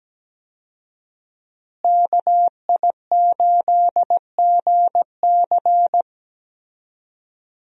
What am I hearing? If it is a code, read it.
Morse code, heard as a keyed tone, about 17 wpm, 700 Hz. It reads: KI8GC
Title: USA Call Signs - 5 Characters - Repeated 3x - 17wpm